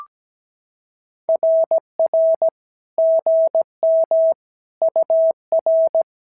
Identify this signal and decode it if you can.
Morse code, heard as a keyed tone, about 17 wpm, 690 Hz. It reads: RR GM UR
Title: SOTA QSO Practice 17wpm